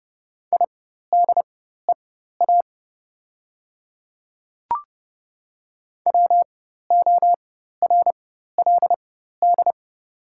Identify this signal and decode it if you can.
Morse code, heard as a keyed tone, about 30 wpm, spaced as for 14 wpm, 700 Hz. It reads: IDEA WORLD
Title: 4x Character Spacing - Top 500 Words - 30wpm